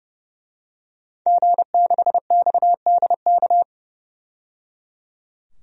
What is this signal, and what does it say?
Morse code, heard as a keyed tone, about 30 wpm, 700 Hz. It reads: G6XDK
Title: DXCC-100 Call Signs - 5 Characters - 30wpm